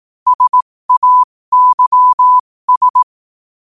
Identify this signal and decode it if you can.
Morse code, heard as a keyed tone, about 18 wpm, spaced as for 15 wpm, 1000 Hz. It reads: SAYS
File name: qotdfiles / 15wpm